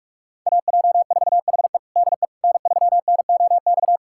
Morse code, heard as a keyed tone, about 46 wpm, 700 Hz. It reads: AJ4HE DE N3NOX
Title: USA Callsign DE Callsign - 5 Characters - 45wpm